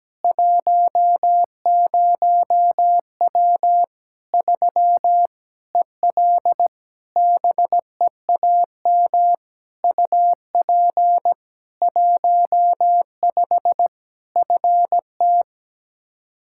Morse code, heard as a keyed tone, about 17 wpm, 700 Hz. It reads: 10W 3 EL BEAM UP 15 FT
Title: Practice QSO - Exchange 2 - Info - 17wpm